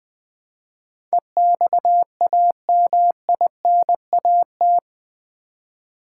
Morse code, heard as a keyed tone, about 20 wpm, 700 Hz. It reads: EXAMINAT
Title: Build A Word - 4 Letters At A Time - 11-Letter Words - 20wpm